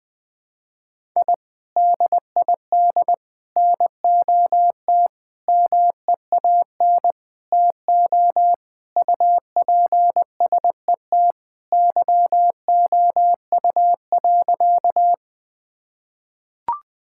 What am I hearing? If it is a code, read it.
Morse code, heard as a keyed tone, about 20 wpm, 700 Hz. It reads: I DID NOT MEAN TO UPSET YOU.
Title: Sentences with 5-Word Phrases - 20wpm